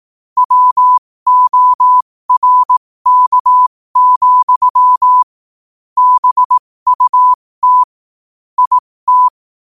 Morse code, heard as a keyed tone, about 18 wpm, spaced as for 15 wpm, 1000 Hz. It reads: WORK, BUT IT
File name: qotdfiles / 15wpm